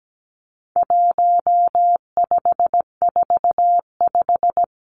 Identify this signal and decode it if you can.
Morse code, heard as a keyed tone, about 17 wpm, 700 Hz. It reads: 1545
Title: Numbers - 4-Digit 17wpm